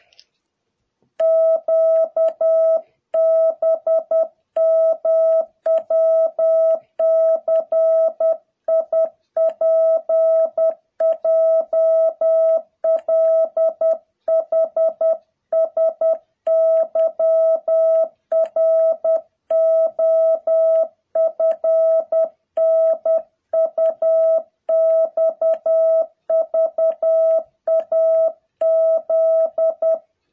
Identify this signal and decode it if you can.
Morse code, heard as a keyed tone, about 10 wpm, 660 Hz. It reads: QBMWCIPJLHSYROFNUXVAZ
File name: jh8cbh / 2021cw